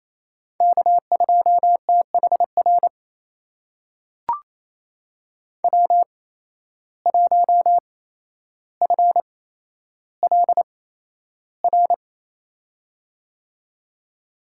Morse code, heard as a keyed tone, about 28 wpm, spaced as for 12 wpm, 700 Hz. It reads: K2THR W1FLR
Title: USA Call Signs - 5 Characters - 8x Character Spacing - 28wpm